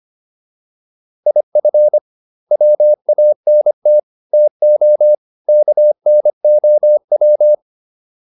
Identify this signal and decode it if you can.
Morse code, heard as a keyed tone, about 25 wpm, 600 Hz. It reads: IF WANT TO KNOW